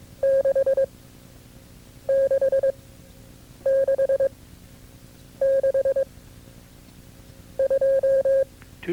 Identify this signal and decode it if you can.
Morse code, heard as a keyed tone, about 22 wpm, 540 Hz. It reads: 6 6 6 6 2
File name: CW-practice/KB6MT-tape2-side1